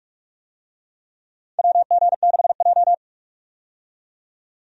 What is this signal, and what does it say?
Morse code, heard as a keyed tone, about 46 wpm, 700 Hz. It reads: WG6J
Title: USA Call Signs - 4 Characters - 45wpm